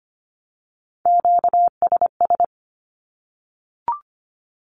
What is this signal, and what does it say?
Morse code, heard as a keyed tone, about 25 wpm, 700 Hz. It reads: QSS
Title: Q-Codes 25wpm